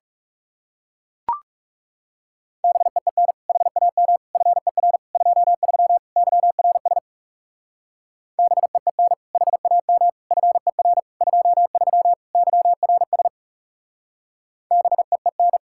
Morse code, heard as a keyed tone, about 36 wpm, 700 Hz. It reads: BEEN HAM FER 23 YRS BEEN HAM FER 23 YRS BEEN
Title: Speed-Racing Common QSO Phrases - 30wpm